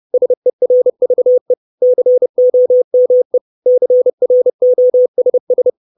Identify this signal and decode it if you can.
Morse code, heard as a keyed tone, about 30 wpm, 500 Hz. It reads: SERVE COME CROSS
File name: 632 Words 30WPM